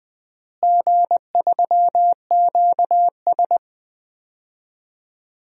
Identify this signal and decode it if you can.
Morse code, heard as a keyed tone, about 20 wpm, 700 Hz. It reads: G3QS
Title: DXCC-100 Call Signs - 4 Characters - Repeated 2x - 20wpm